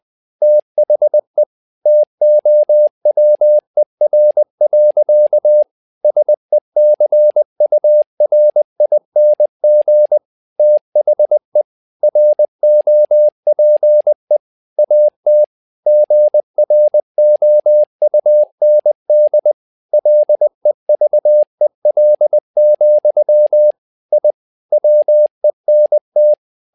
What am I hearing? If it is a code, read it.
Morse code, heard as a keyed tone, about 20 wpm, 600 Hz. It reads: THE TOWER. SECURING THE ROPE AT GROUND LEVEL, I WENT
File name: tower trouble 20wpm